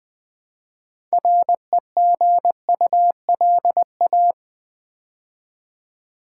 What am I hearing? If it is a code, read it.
Morse code, heard as a keyed tone, about 20 wpm, 700 Hz. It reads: REGULA
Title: Build A Word - 2 Letters At A Time - 8-Letter Words - 20wpm